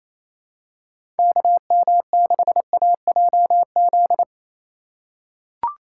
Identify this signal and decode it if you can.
Morse code, heard as a keyed tone, about 28 wpm, 700 Hz. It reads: KM6AJZ
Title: USA Call Signs - 3x Character Spacing - 28wpm